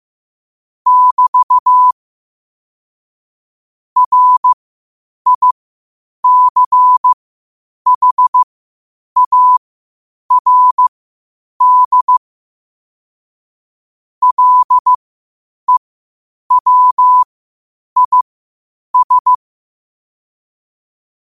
Morse code, heard as a keyed tone, about 15 wpm, spaced as for 8 wpm, 1000 Hz. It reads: <BT> RICHARD LEWIS